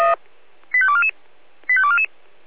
Five-tone selective call (ZVEI), heard as quick runs of five tones, tones of about 70 ms. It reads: 85244 85244
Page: ZVEI